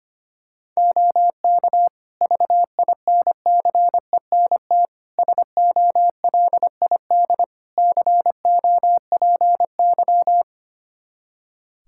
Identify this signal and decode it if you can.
Morse code, heard as a keyed tone, about 25 wpm, 700 Hz. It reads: OK VINCENT SOLID COPY
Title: Practice QSO - Exchange 3 - More Info - 25wpm